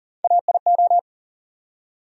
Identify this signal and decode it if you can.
Morse code, heard as a keyed tone, about 40 wpm, 700 Hz. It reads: AIO
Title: USA Call Sign Postfixes - 3 Letters 40wpm